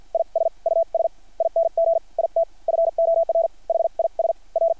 Morse code, heard as a keyed tone, about 50 wpm, spaced as for 39 wpm, 660 Hz. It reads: IS USING IT FOR HIS R
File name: archived / CRSnet